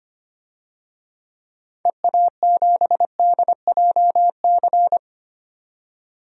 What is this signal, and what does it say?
Morse code, heard as a keyed tone, about 25 wpm, 700 Hz. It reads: EA7DJC
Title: DXCC-100 Call Signs - 6 Characters - 25wpm